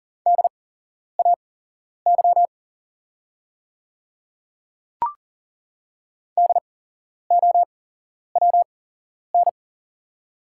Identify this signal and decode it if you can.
Morse code, heard as a keyed tone, about 40 wpm, spaced as for 11 wpm, 690 Hz. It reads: DAY DOWN
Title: 8x Character Spacing - Top 100 Words - 40wpm